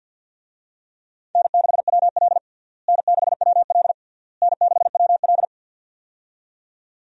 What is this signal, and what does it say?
Morse code, heard as a keyed tone, about 51 wpm, 700 Hz. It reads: N6WL N6WL N6WL
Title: USA Call Signs - 4 Characters - Repeated 3x - 50wpm